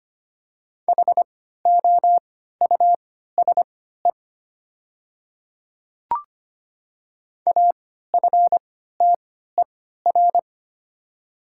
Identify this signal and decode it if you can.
Morse code, heard as a keyed tone, about 25 wpm, spaced as for 14 wpm, 700 Hz. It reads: HOUSE AFTER
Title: 3x Character Spacing - Top 200 Words - 25wpm